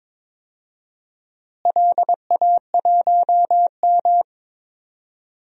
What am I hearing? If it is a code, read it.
Morse code, heard as a keyed tone, about 22 wpm, 700 Hz. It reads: LA1M
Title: DXCC-100 Call Signs - 4 Characters - Repeated 3x - 22wpm